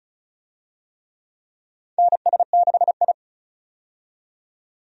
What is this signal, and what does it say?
Morse code, heard as a keyed tone, about 36 wpm, 700 Hz. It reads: NS6I
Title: USA Call Signs - 4 Characters - 35wpm